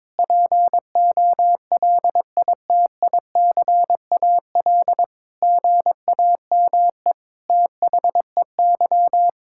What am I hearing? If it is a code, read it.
Morse code, heard as a keyed tone, about 22 wpm, 700 Hz. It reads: POLITICAL GAME THEY
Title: Sentences from Top 1000 Words - Review Part 5 - 22wpm